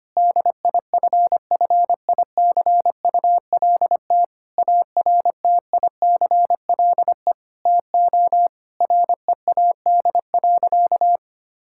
Morse code, heard as a keyed tone, about 25 wpm, 700 Hz. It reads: DIFFICULT ARTICLE TO READ.
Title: Sentences from Top 1000 Words - Review Part 5 - 25wpm